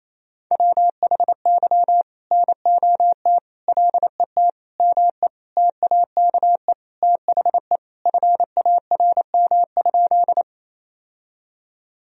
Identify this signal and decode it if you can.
Morse code, heard as a keyed tone, about 28 wpm, 700 Hz. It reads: WHY NOT LET ME TAKE THE FARM?
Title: Sentences from Top 1000 Words - Review Part 2 - 28wpm